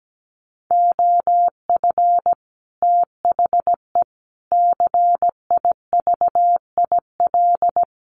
Morse code, heard as a keyed tone, about 17 wpm, 700 Hz. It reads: OF THE CIVIL
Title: Sets of 5 Words Encore2 17wpm